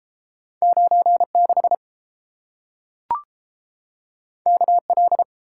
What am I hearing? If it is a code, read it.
Morse code, heard as a keyed tone, about 30 wpm, 700 Hz. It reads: 96 KL